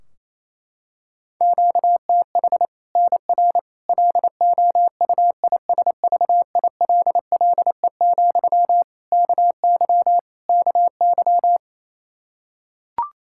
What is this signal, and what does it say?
Morse code, heard as a keyed tone, about 28 wpm, 700 Hz. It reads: QTH NR LOUISVILLE, KY KY
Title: K1USN SST QSO Practice 28wpm